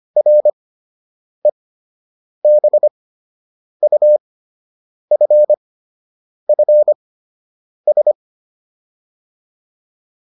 Morse code, heard as a keyed tone, about 25 wpm, spaced as for 8 wpm, 600 Hz. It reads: REBUFFS